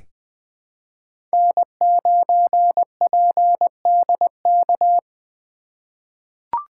Morse code, heard as a keyed tone, about 21 wpm, 700 Hz. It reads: N9PDK
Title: USA Call Signs - 5 Characters - 2x Character Spacing - 20wpm